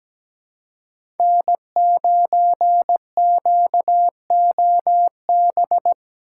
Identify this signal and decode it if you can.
Morse code, heard as a keyed tone, about 17 wpm, 700 Hz. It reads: N9QOB
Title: USA Call Signs - 2x Character Spacing - 17wpm